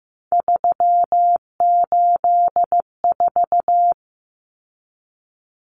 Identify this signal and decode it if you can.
Morse code, heard as a keyed tone, about 15 wpm, 700 Hz. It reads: 384
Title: Common Cut Numbers - 3-Digit 15wpm